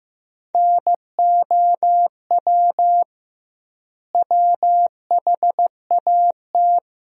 Morse code, heard as a keyed tone, about 16 wpm, 700 Hz. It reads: NOW WHAT
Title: 2x Word-Spacing - Sentences from Top 100 Words - 15wpm